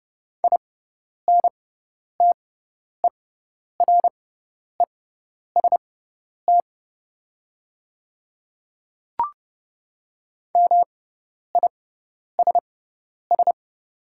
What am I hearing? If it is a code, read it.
Morse code, heard as a keyed tone, about 30 wpm, spaced as for 10 wpm, 700 Hz. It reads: INTEREST MISS
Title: 6x Character Spacing - Top 500 Words - 30wpm